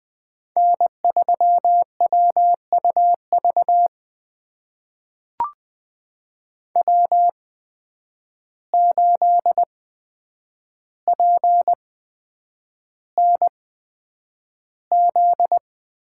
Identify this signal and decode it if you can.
Morse code, heard as a keyed tone, about 20 wpm, spaced as for 9 wpm, 700 Hz. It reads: N3WUV W8PNZ
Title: USA Call Signs - 5 Characters - 8x Character Spacing - 20wpm